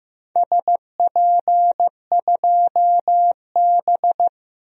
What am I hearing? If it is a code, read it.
Morse code, heard as a keyed tone, about 15 wpm, 700 Hz. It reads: SP2B
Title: DXCC-100 Call Signs - 4 Characters - Repeated 2x - 15wpm